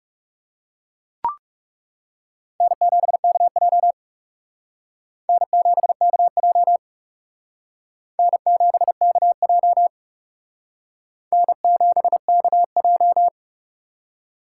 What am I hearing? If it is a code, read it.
Morse code, heard as a keyed tone, about 37 wpm, 700 Hz. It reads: N7KJ N7KJ N7KJ N7KJ
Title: Speed Racing - USA Call Signs 30wpm